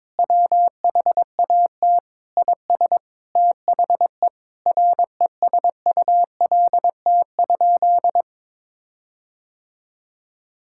Morse code, heard as a keyed tone, about 22 wpm, 700 Hz. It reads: WHAT IS THE RESULT?
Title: Sentences from Top 1600 Words 22wpm